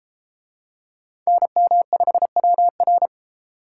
Morse code, heard as a keyed tone, about 31 wpm, 700 Hz. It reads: NM5WR